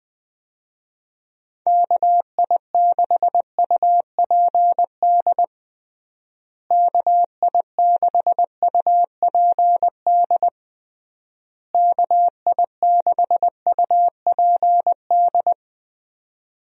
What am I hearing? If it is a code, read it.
Morse code, heard as a keyed tone, about 20 wpm, 700 Hz. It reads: KI6UPD KI6UPD KI6UPD
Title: USA Call Signs - 6 Characters - Repeated 3x - 20wpm